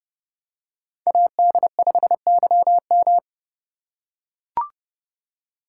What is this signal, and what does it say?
Morse code, heard as a keyed tone, about 30 wpm, 700 Hz. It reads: AD5YM